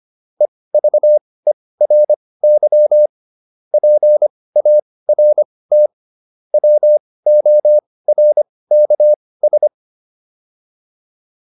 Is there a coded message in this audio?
Yes: Morse code from 0.4 to 9.7 s, heard as a keyed tone, about 25 wpm, spaced as for 18 wpm, 600 Hz. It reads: EVERY PART WORKS